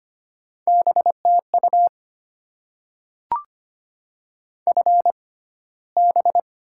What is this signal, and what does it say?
Morse code, heard as a keyed tone, about 25 wpm, spaced as for 15 wpm, 700 Hz. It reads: BTU FB
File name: Top 20 QSO Elements - 6x Character Spacing - 25wpm